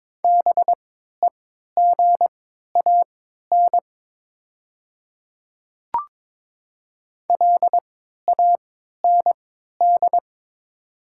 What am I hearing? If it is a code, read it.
Morse code, heard as a keyed tone, about 22 wpm, spaced as for 13 wpm, 700 Hz. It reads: BEGAN LAND
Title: 3x Character Spacing - Top 500 Words - 22wpm